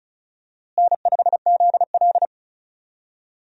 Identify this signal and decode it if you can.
Morse code, heard as a keyed tone, about 36 wpm, 700 Hz. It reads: N5ZL
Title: USA Call Signs - 4 Characters - Repeated 2x - 35wpm